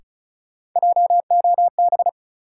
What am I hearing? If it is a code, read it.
Morse code, heard as a keyed tone, about 36 wpm, 700 Hz. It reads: JOB
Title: Sentences from Top 600 Words - New Word Drill - 35wpm